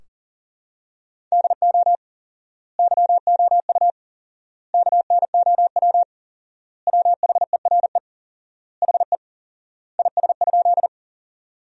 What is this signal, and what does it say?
Morse code, heard as a keyed tone, about 37 wpm, 700 Hz. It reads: DO YOU KNOW WHERE HE IS?
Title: 4x Word-Spacing - Sentences from Top 300 Words - 40wpm